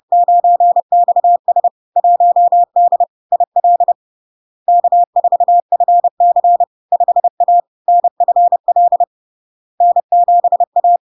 Morse code, heard as a keyed tone, about 30 wpm, 700 Hz. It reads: 9XS 1D IL K4FC 5A NFL N7A
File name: LARK-FD-2022-Log-in-CW-30wpm-700-Hz